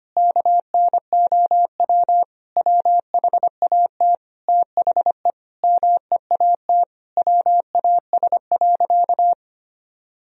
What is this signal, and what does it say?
Morse code, heard as a keyed tone, about 25 wpm, 700 Hz. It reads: KNOW WHAT THE MEAT WAS.
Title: Sentences from Top 1000 Words - Review Part 3 - 25wpm